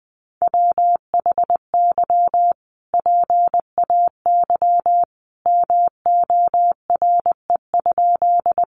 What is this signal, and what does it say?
Morse code, heard as a keyed tone, about 20 wpm, 700 Hz. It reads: WHY PAY MORE?
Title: Sentences from Top 700 Words 20wpm